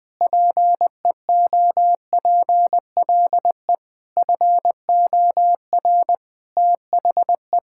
Morse code, heard as a keyed tone, about 20 wpm, 700 Hz. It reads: PEOPLE FOR THE
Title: Sentences with 8-Word Phrases - 20wpm